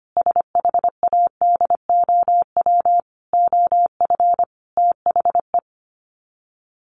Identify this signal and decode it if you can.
Morse code, heard as a keyed tone, about 25 wpm, 700 Hz. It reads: SHADOW OF THE
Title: Sets of 5 Words Encore 25wpm